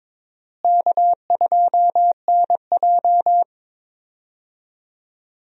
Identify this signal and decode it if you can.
Morse code, heard as a keyed tone, about 23 wpm, 700 Hz. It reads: K2NJ
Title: USA Call Signs - 4 Characters - Repeated 2x - 22wpm